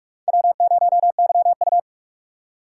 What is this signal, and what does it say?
Morse code, heard as a keyed tone, about 46 wpm, 700 Hz. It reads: W0YU
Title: USA Call Signs - 4 Characters - 4x Character Spacing - 45wpm